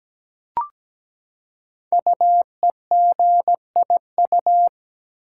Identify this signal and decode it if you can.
Morse code, heard as a keyed tone, about 17 wpm, 700 Hz. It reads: UEGIU
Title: Code Groups - 5 Letters - 17wpm